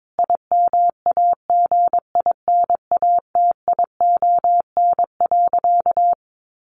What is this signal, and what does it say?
Morse code, heard as a keyed tone, about 22 wpm, 700 Hz. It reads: IMAGINATION.
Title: Sentences from Top 1300 Words 22wpm